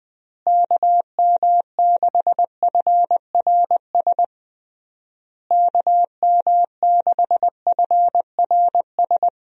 Morse code, heard as a keyed tone, about 20 wpm, 700 Hz. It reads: KM6FRS KM6FRS
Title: USA Call Signs - 6 Characters - Repeated 3x - 20wpm